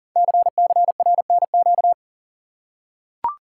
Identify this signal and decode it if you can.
Morse code, heard as a keyed tone, about 43 wpm, 700 Hz. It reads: CCRNQ
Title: Code Groups - 5 Letter:Numbers - 40wpm